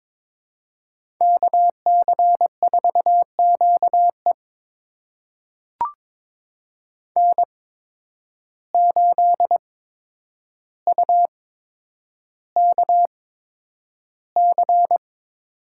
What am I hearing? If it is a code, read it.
Morse code, heard as a keyed tone, about 22 wpm, spaced as for 9 wpm, 700 Hz. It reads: KC4QE N8UKC
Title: USA Call Signs - 5 Characters - 8x Character Spacing - 22wpm